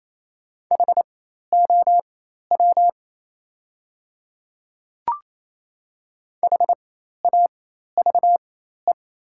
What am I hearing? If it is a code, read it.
Morse code, heard as a keyed tone, about 28 wpm, spaced as for 13 wpm, 700 Hz. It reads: HOW HAVE
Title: 4x Character Spacing - Top 200 Words - 28wpm